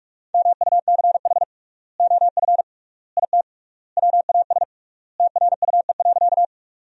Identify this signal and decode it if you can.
Morse code, heard as a keyed tone, about 41 wpm, 700 Hz. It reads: MUCH OF IT WAS TRUE.
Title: 3x Word-Spacing - Sentences from Top 400 Words - 45wpm